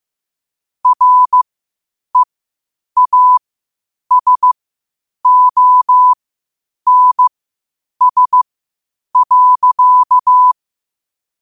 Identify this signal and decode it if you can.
Morse code, heard as a keyed tone, about 15 wpm, spaced as for 8 wpm, 1000 Hz. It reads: REASONS.
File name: 8wpm